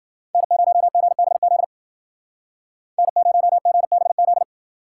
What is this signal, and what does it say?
Morse code, heard as a keyed tone, about 53 wpm, 700 Hz. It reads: N0GBZ N0GBZ